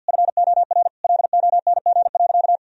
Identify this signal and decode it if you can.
Morse code, heard as a keyed tone, about 48 wpm, 700 Hz. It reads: FOR LONG.
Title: Sentences from Top 1000 Words - Review Part 1 - 50wpm